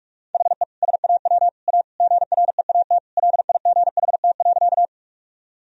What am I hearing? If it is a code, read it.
Morse code, heard as a keyed tone, about 43 wpm, 700 Hz. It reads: HE SAW A GREAT LIGHT.
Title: Sentences from Top 1000 Words - Review Part 1 - 45wpm